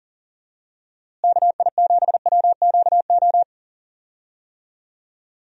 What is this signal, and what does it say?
Morse code, heard as a keyed tone, about 40 wpm, 700 Hz. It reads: KI7WQO
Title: USA Call Signs - 6 Characters - Repeated 3x - 40wpm